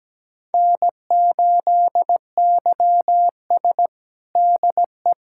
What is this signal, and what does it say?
Morse code, heard as a keyed tone, about 17 wpm, 700 Hz. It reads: N8YS DE
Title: USA Callsign DE Callsign - 4 Characters - 17wpm